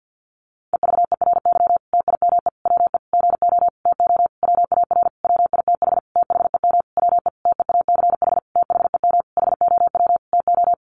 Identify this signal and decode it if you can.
Morse code, heard as a keyed tone, about 50 wpm, 700 Hz. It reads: EVERY TIME WE GO TO WAR WITH THEM WE TEACH THEM HOW TO
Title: Sentences from Top 1300 Words 50wpm